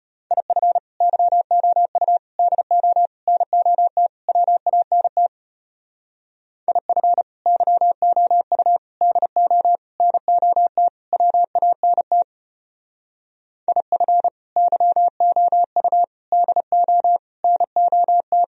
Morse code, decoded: IF YOU DO NOT WANT IF YOU DO NOT WANT IF YOU DO NOT

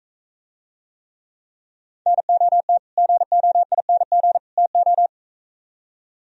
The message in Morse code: NOT GOING TO

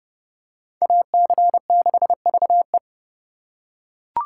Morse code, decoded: AC6VE

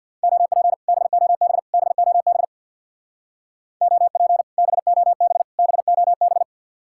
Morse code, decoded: OP BOB BOB OP BOB BOB

